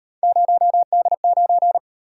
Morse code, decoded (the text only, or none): 0D9